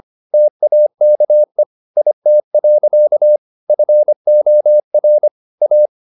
Morse code, heard as a keyed tone, about 25 wpm, 600 Hz. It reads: TAKE IT. FOR A
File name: the old man and the sea part 1 25wpm